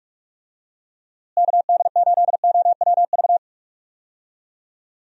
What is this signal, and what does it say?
Morse code, heard as a keyed tone, about 41 wpm, 700 Hz. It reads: KD8OWV
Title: USA Call Signs - 6 Characters - Repeated 2x - 45wpm